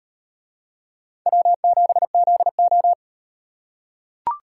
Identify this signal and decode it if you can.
Morse code, heard as a keyed tone, about 38 wpm, 700 Hz. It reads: W7ZO